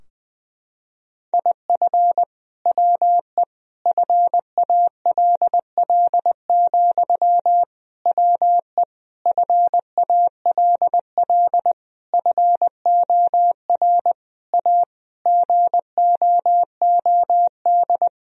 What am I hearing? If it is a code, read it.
Morse code, heard as a keyed tone, about 20 wpm, 700 Hz. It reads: IF WE FALL, WE FALL FOR A GOOD
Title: Sentences from Top 1000 Words - Review Part 2 - 20wpm